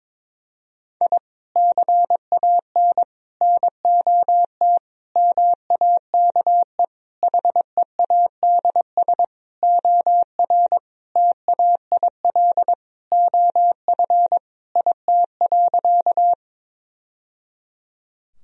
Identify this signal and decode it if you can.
Morse code, heard as a keyed tone, about 21 wpm, 700 Hz. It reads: I CAN NOT MAKE HEADS OR TAIL OF IT.
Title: Sentences from Top 1000 Words - Review Part 3 - 22wpm